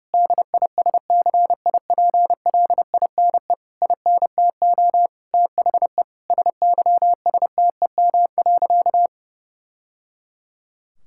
Morse code, heard as a keyed tone, about 31 wpm, 700 Hz. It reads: DISCIPLINE INTO THE SYSTEM.
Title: Sentences from Top 1000 Words - Review Part 6 - 30wpm